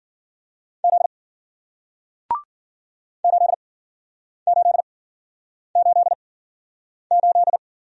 Morse code, heard as a keyed tone, about 50 wpm, spaced as for 9 wpm, 700 Hz. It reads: Z 8888
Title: Speed-Racing Single Letter-Number 40wpm